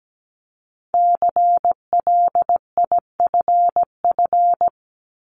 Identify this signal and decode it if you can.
Morse code, heard as a keyed tone, about 17 wpm, 700 Hz. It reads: CLIFF
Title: Multi-letter Phonemes Repeated 3x 17wpm